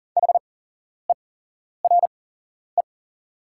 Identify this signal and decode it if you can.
Morse code, heard as a keyed tone, about 40 wpm, spaced as for 11 wpm, 700 Hz. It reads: HERE
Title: 8x Character Spacing - Top 500 Words - 40wpm